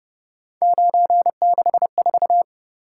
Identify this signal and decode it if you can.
Morse code, heard as a keyed tone, about 30 wpm, 700 Hz. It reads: 964
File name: Speed-Racing - Three Letter-Number Combinations - 25wpm